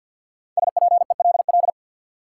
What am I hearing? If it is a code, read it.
Morse code, heard as a keyed tone, about 52 wpm, 700 Hz. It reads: SPELL
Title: Sentences from Top 1000 Words - Review Part 1 - New Word Drill - 50wpm